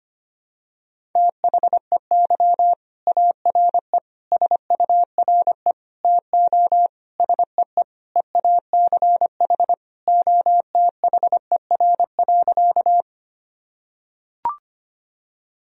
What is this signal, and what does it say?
Morse code, heard as a keyed tone, about 25 wpm, 700 Hz. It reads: THEY ARE SURE TO SEE EACH OTHER.
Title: Sentences from Top 1000 Words - Review Part 1 - 25wpm